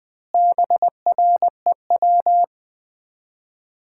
Morse code, heard as a keyed tone, about 20 wpm, 700 Hz. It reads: BREW
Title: Common QSO Phrases - 3x Word-Spacing - 20wpm